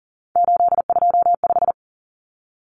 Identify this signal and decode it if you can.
Morse code, heard as a keyed tone, about 40 wpm, 700 Hz. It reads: 825
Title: Common Cut Numbers - 3-Digit 40wpm